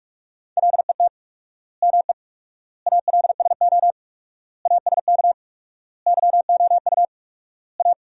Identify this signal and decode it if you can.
Morse code, heard as a keyed tone, about 46 wpm, 700 Hz. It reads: LET ME ALSO ASK YOU A
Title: 4x Word-Spacing - Sentences from Top 400 Words - 45wpm